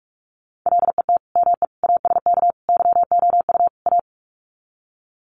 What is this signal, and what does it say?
Morse code, heard as a keyed tone, about 44 wpm, 700 Hz. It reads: LET ME ASK YOU A
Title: Sets of 5 Words Encore 45wpm